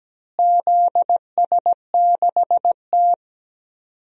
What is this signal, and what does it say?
Morse code, heard as a keyed tone, about 17 wpm, 700 Hz. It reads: ZS6T
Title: DXCC-100 Call Signs - 4 Characters - Repeated 3x - 17wpm